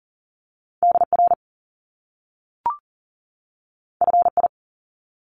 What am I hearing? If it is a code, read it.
Morse code, heard as a keyed tone, about 41 wpm, 700 Hz. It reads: DR FI
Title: Top Two Letter Combinations in Words 40wpm